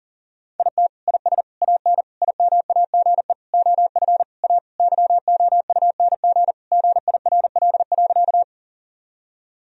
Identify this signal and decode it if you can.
Morse code, heard as a keyed tone, about 37 wpm, 700 Hz. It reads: IT IS AN IMAGE OF A YOUNG GIRL.
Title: Sentences from Top 1000 Words - Review Part 4 - 40wpm